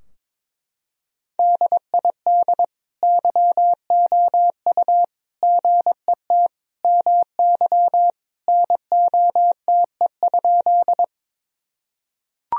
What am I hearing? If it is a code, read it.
Morse code, heard as a keyed tone, about 22 wpm, 700 Hz. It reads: DID YOU GET MY NOTE?
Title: Sentences from Top 1000 Words - Review Part 2 - 22wpm